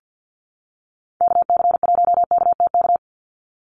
Morse code, heard as a keyed tone, about 50 wpm, 700 Hz. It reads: KC1KTK
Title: USA Call Signs 50wpm